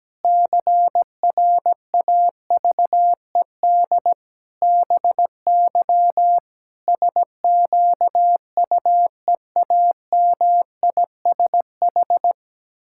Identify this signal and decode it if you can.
Morse code, heard as a keyed tone, about 17 wpm, 700 Hz. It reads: CRAVED BY SQUEAMISH
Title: Pangram Challenge v2 - 17wpm